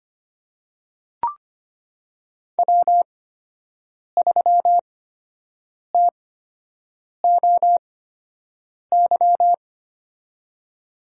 Morse code, heard as a keyed tone, about 25 wpm, spaced as for 7 wpm, 700 Hz. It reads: W3TOY
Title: USA Call Signs - 8x Character Spacing - 25wpm